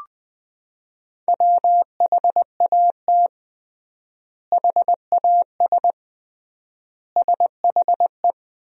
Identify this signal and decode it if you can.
Morse code, heard as a keyed tone, about 21 wpm, 700 Hz. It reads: WHAT HAS SHE
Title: 3x Word-Spacing - Sentences from Top 100 Words - 20wpm